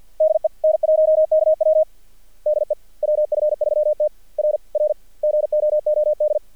Morse code, heard as a keyed tone, about 56 wpm, 620 Hz. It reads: DE N1KW DE WF3T R R GOOD
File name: archived / CRSnet